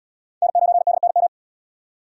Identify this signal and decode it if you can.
Morse code, heard as a keyed tone, about 72 wpm, 700 Hz. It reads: N9DNM